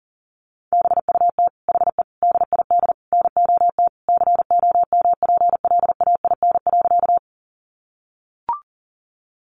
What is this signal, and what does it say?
Morse code, heard as a keyed tone, about 40 wpm, 700 Hz. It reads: BUT HE DID NOT COMPLAIN.
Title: Sentences from Top 1300 Words 40wpm